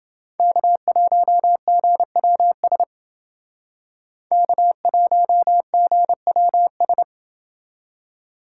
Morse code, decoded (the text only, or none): K1GWS K1GWS